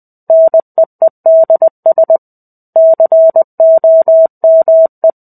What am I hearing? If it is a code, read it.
Morse code, heard as a keyed tone, about 20 wpm, 650 Hz. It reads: NEEDS COME
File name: 05-Chapter-05 20 WPM